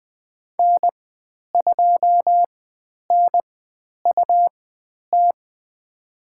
Morse code, heard as a keyed tone, about 20 wpm, spaced as for 10 wpm, 700 Hz. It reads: N2NUT
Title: 102 Lesson 17g - Letter u - Call Signs 20wpm